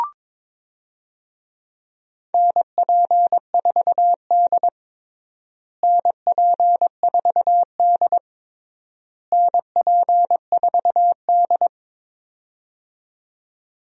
Morse code, heard as a keyed tone, about 22 wpm, 700 Hz. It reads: NP4D NP4D NP4D
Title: USA Call Signs - 4 Characters - Repeated 3x - 22wpm